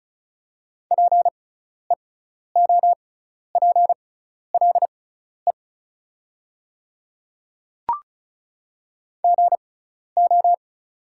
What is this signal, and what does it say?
Morse code, heard as a keyed tone, about 36 wpm, spaced as for 12 wpm, 700 Hz. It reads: PEOPLE GO
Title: 6x Character Spacing - Top 200 Words - 35wpm